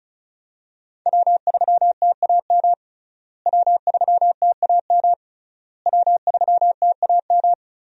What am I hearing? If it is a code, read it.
Morse code, heard as a keyed tone, about 34 wpm, 700 Hz. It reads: W3TAM W3TAM W3TAM
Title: USA Call Signs - 5 Characters - Repeated 3x - 35wpm